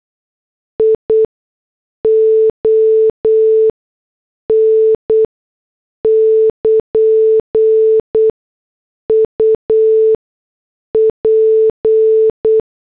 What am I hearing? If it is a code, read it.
Morse code, heard as a keyed tone, about 8 wpm, 440 Hz. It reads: I O N <KN> U P
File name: morse / headlines